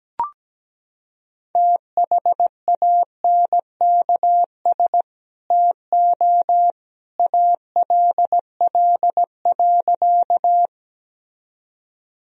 Morse code, heard as a keyed tone, about 17 wpm, 700 Hz. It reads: THANKS TO ALL.
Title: Sentences from Top 1000 Words - Review Part 3 - 17wpm